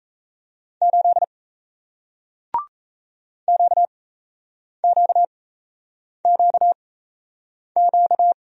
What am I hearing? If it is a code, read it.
Morse code, heard as a keyed tone, about 38 wpm, spaced as for 8 wpm, 700 Hz. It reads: 8 QQQQ